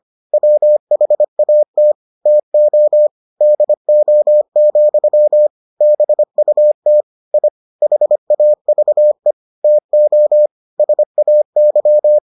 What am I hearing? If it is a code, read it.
Morse code, heard as a keyed tone, about 25 wpm, 600 Hz. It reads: WHAT TO DO, BUT I HAVE TO SAY